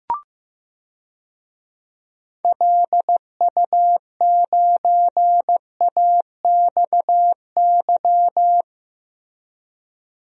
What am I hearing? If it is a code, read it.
Morse code, heard as a keyed tone, about 15 wpm, 700 Hz. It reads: LU9AXY
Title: DXCC-100 Call Signs - 6 Characters - 15wpm